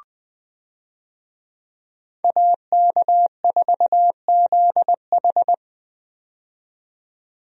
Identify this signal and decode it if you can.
Morse code, heard as a keyed tone, about 20 wpm, 700 Hz. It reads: AK4ZH
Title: USA Call Signs - 5 Characters - 20wpm